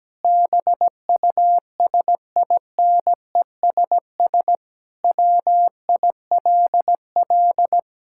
Morse code, heard as a keyed tone, about 17 wpm, 700 Hz. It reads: BUSINESS WILL
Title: Sentences from Top 1000 Words - Review Part 5 - 17wpm